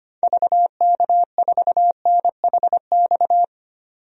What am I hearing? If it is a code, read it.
Morse code, heard as a keyed tone, about 25 wpm, 700 Hz. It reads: VK4NHX
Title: DXCC-100 Call Signs - 6 Characters - 25wpm